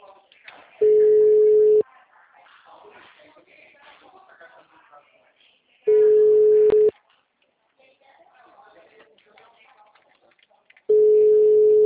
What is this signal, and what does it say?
Ringback tone (Europe, 425 Hz), with speech.
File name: out- -2004- - - .17186